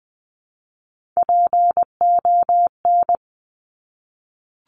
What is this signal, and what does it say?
Morse code, heard as a keyed tone, about 20 wpm, 700 Hz. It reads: PON
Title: Top Three Letter Combinations in Words 20wpm